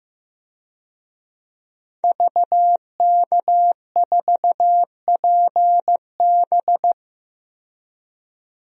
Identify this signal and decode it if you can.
Morse code, heard as a keyed tone, about 15 wpm, 700 Hz. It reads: VK4PB
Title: DXCC-100 Call Signs - 5 Characters - 15wpm